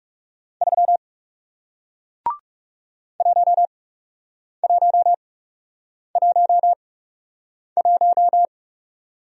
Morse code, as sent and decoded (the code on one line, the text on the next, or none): ...-- / .---- .---- .---- .----
3 1111